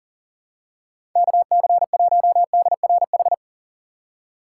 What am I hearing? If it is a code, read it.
Morse code, heard as a keyed tone, about 37 wpm, 700 Hz. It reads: KC1DRH
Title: USA Call Signs - 6 Characters - 6x Character Spacing - 40wpm